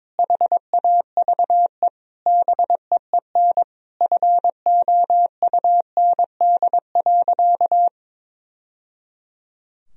Morse code, heard as a keyed tone, about 22 wpm, 700 Hz. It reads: HAVE BEEN FOUND.
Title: Sentences from Top 1000 Words - Review Part 4 - 22wpm